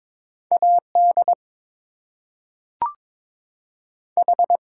Morse code, heard as a keyed tone, about 22 wpm, 700 Hz. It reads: AD H
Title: Two Letter Words - 22wpm